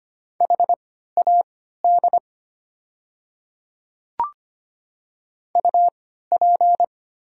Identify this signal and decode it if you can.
Morse code, heard as a keyed tone, about 25 wpm, spaced as for 14 wpm, 700 Hz. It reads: HAD UP
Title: 3x Character Spacing - Top 100 Words - 25wpm